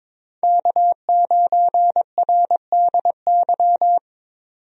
Morse code, heard as a keyed tone, about 23 wpm, 700 Hz. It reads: K9RDY
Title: USA Call Signs - 5 Characters - 22wpm